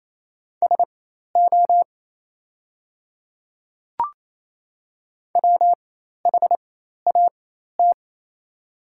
Morse code, heard as a keyed tone, about 28 wpm, spaced as for 13 wpm, 700 Hz. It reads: SO WHAT
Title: 4x Character Spacing - Top 300 Words - 28wpm